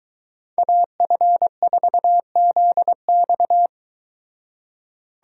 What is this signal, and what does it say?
Morse code, heard as a keyed tone, about 22 wpm, 700 Hz. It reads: AF4ZX